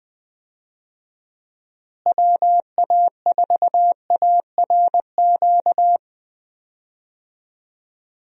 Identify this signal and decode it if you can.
Morse code, heard as a keyed tone, about 21 wpm, 700 Hz. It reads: WA4ARQ
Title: USA Call Signs - 6 Characters - Repeated 3x - 20wpm